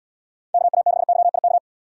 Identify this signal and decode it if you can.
Morse code, heard as a keyed tone, about 79 wpm, 690 Hz. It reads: KA6YIK